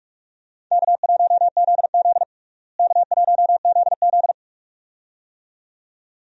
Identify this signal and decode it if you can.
Morse code, heard as a keyed tone, about 46 wpm, 700 Hz. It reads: K1ZZ K1ZZ
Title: USA Call Signs - 4 Characters - Repeated 2x - 45wpm